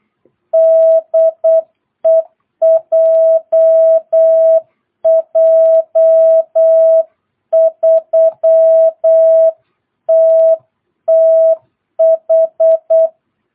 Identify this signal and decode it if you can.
Morse code, heard as a keyed tone, about 8 wpm, 660 Hz. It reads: DE JJ3TTH
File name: jh8cbh / 2022cw